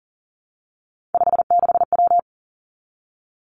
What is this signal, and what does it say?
Morse code, heard as a keyed tone, about 40 wpm, 700 Hz. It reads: 56W
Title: Three Challenging Letter-Number Combinations 40wpm